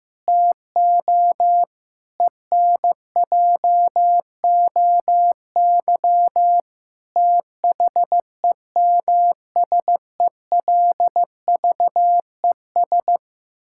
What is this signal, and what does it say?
Morse code, heard as a keyed tone, about 15 wpm, 700 Hz. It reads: TO ENJOY THEMSELVES
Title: Sentences from Top 1000 Words - Review Part 5 - 15wpm